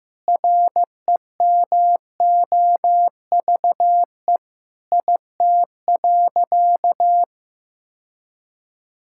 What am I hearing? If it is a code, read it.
Morse code, heard as a keyed tone, about 15 wpm, 700 Hz. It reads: REMOVE IT.
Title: Sentences from Top 1000 Words - Review Part 5 - 15wpm